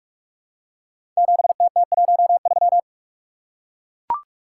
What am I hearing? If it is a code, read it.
Morse code, heard as a keyed tone, about 44 wpm, 700 Hz. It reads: 7TT13
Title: Code Groups - 5 Letter:Numbers - 45wpm